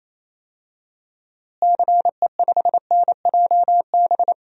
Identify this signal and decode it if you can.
Morse code, heard as a keyed tone, about 28 wpm, 700 Hz. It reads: CE5NJB
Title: DXCC-100 Call Signs - 6 Characters - Repeated 3x - 28wpm